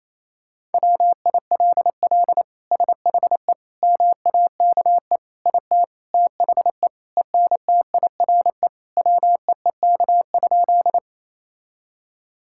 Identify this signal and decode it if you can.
Morse code, heard as a keyed tone, about 28 wpm, 700 Hz. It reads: WILL SHE MAKE IT THE ENTIRE WEEK?
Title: Sentences from Top 1000 Words - Review Part 5 - 28wpm